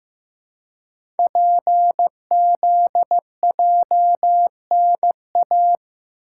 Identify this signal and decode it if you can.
Morse code, heard as a keyed tone, about 15 wpm, 690 Hz. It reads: PZJNA
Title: Code Groups - 5 Letters - 15wpm